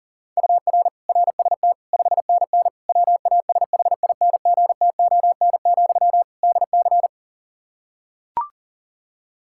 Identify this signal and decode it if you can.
Morse code, heard as a keyed tone, about 40 wpm, 700 Hz. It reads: UR RST 5NN WASHINGTON, DC
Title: Practice QSO - Exchange 1 - Beginning - 40wpm